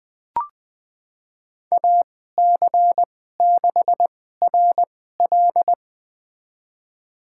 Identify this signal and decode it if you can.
Morse code, heard as a keyed tone, about 20 wpm, spaced as for 14 wpm, 700 Hz. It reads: AC6RL
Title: USA Call Signs - 5 Characters - 2x Character Spacing - 20wpm